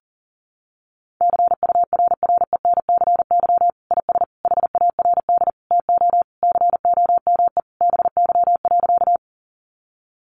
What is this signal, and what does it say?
Morse code, heard as a keyed tone, about 40 wpm, 700 Hz. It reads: CURRENCY IS HARD TO COME BY.
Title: Sentences from Top 1500 Words 40wpm